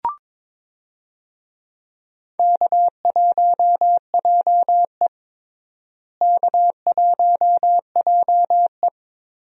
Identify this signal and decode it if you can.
Morse code, heard as a keyed tone, about 21 wpm, 700 Hz. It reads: K1JE K1JE
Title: USA Call Signs - 4 Characters - Repeated 3x - 22wpm